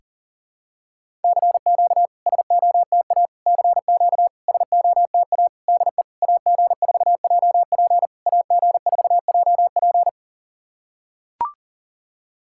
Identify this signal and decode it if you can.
Morse code, heard as a keyed tone, about 40 wpm, 690 Hz. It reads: CQ SOTA CQ SOTA DE AG4JP AG4JP
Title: SOTA QSO Practice 40wpm